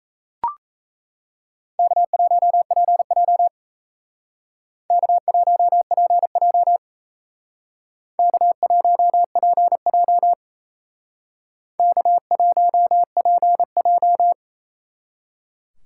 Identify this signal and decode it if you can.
Morse code, heard as a keyed tone, about 36 wpm, 700 Hz. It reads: K1PJ K1PJ K1PJ K1PJ